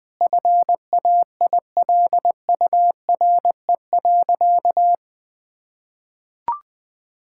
Morse code, heard as a keyed tone, about 21 wpm, 700 Hz. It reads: FAILURE.
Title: Sentences from Top 1000 Words - Review Part 6 - 20wpm